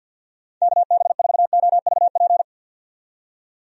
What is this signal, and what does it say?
Morse code, heard as a keyed tone, about 52 wpm, 700 Hz. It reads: KB4OVP
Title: USA Call Signs - 6 Characters - 50wpm